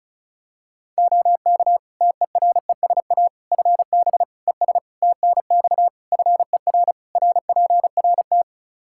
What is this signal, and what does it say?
Morse code, heard as a keyed tone, about 36 wpm, 700 Hz. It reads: OK TERESA FB ES TNX FER RPRT
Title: Practice QSO - Exchange 2 - Info - 35wpm